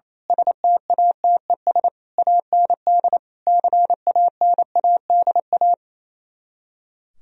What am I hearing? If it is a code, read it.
Morse code, heard as a keyed tone, about 27 wpm, 700 Hz. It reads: STATES AND CANADA
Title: Speed-Racing - Sets of 5 Words 28wpm